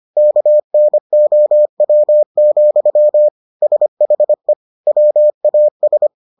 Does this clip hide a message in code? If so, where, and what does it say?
Yes: Morse code from 0.2 to 6.1 s, heard as a keyed tone, about 25 wpm, 600 Hz. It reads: KNOW, SHE WAS